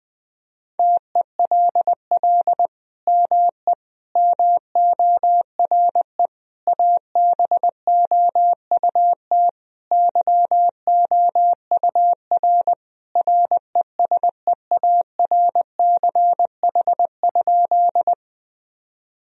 Morse code, heard as a keyed tone, about 20 wpm, 700 Hz. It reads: TELL ME MORE ABOUT YOUR RESEARCH?
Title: Sentences from Top 1000 Words - Review Part 4 - 20wpm